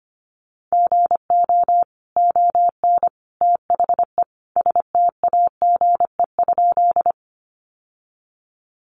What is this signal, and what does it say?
Morse code, heard as a keyed tone, about 25 wpm, 700 Hz. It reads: GO ON THE STAGE?
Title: Sentences from Top 1200 Words 25wpm